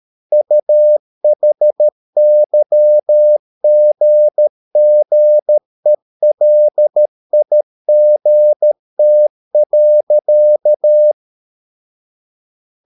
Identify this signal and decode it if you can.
Morse code, heard as a keyed tone, about 13 wpm, 600 Hz. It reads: UHYGGELIGT.
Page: Ferie 13 wpm